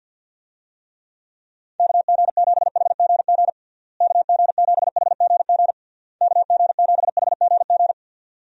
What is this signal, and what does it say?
Morse code, heard as a keyed tone, about 50 wpm, 700 Hz. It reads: KG7HGG KG7HGG KG7HGG
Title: USA Call Signs - 6 Characters - Repeated 3x - 50wpm